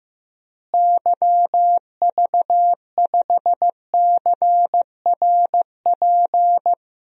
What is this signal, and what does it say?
Morse code, heard as a keyed tone, about 15 wpm, 700 Hz. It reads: YV5CRP
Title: DXCC-100 Call Signs - 6 Characters - Repeated 3x - 15wpm